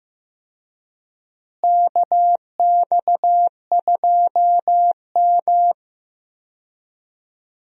Morse code, heard as a keyed tone, about 15 wpm, 700 Hz. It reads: KX2M
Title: USA Call Signs - 4 Characters - Repeated 3x - 15wpm